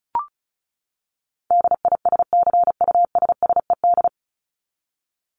Morse code, heard as a keyed tone, about 34 wpm, 700 Hz. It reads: DISCUSSED
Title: ED Suffix - Top 100 Words with Base Words 35wpm